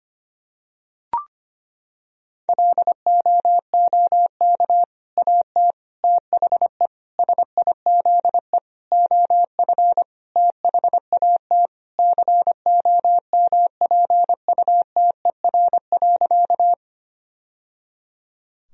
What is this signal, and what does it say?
Morse code, heard as a keyed tone, about 25 wpm, 700 Hz. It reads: LOOK AT THE SIZE OF THAT COMPUTER.
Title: Sentences from Top 1000 Words - Review Part 3 - 25wpm